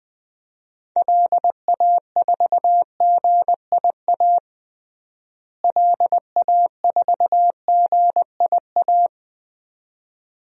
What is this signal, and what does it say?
Morse code, heard as a keyed tone, about 21 wpm, 700 Hz. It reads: LA4GIA LA4GIA
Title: DXCC-100 Call Signs - 6 Characters - Repeated 2x - 20wpm